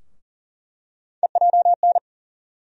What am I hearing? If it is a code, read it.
Morse code, heard as a keyed tone, about 40 wpm, 700 Hz. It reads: EJN
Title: USA Call Sign Postfixes - 3 Letters 40wpm